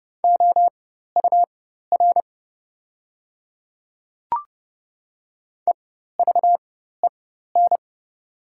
Morse code, heard as a keyed tone, about 29 wpm, spaced as for 14 wpm, 690 Hz. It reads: OUR EVEN
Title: 4x Character Spacing - Top 200 Words - 30wpm